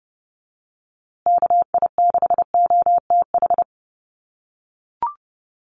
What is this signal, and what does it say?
Morse code, heard as a keyed tone, about 30 wpm, 700 Hz. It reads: KI6OTH
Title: USA Call Signs Repeated 2x 30wpm